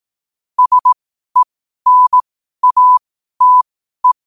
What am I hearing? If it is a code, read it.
Morse code, heard as a keyed tone, about 18 wpm, spaced as for 12 wpm, 1000 Hz. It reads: SENATE
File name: qotdfiles / 12wpm